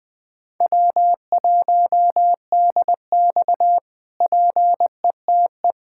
Morse code, heard as a keyed tone, about 21 wpm, 700 Hz. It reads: W1DX PETE
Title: K1USN SST QSO Practice - 20wpm